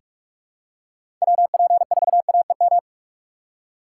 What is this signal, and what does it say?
Morse code, heard as a keyed tone, about 46 wpm, 700 Hz. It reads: WP4AEM
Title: USA Call Signs - 6 Characters - 2x Character Spacing - 45wpm